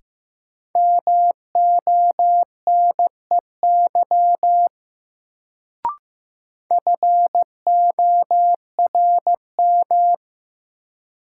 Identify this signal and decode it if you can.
Morse code, heard as a keyed tone, about 15 wpm, 700 Hz. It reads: MONEY FORM
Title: Sets of 4 Words - New Word Drill - 15wpm